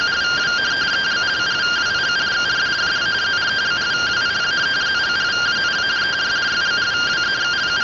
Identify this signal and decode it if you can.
Two-tone teleprinter signal, about 170 Hz shift.